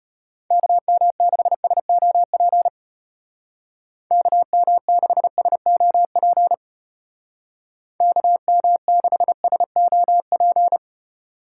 Morse code, heard as a keyed tone, about 32 wpm, 700 Hz. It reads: KM6SOP KM6SOP KM6SOP